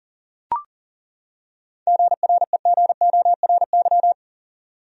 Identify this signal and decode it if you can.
Morse code, heard as a keyed tone, about 40 wpm, 700 Hz. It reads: GREGORY
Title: USA Common Names - 40wpm